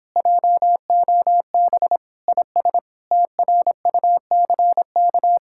Morse code, heard as a keyed tone, about 29 wpm, 700 Hz. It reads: JOB IS TRUCK